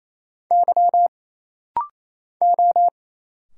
Morse code, heard as a keyed tone, about 27 wpm, 700 Hz. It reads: Y O
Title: Single Letter-Number Alt - Rapid-Fire 28wpm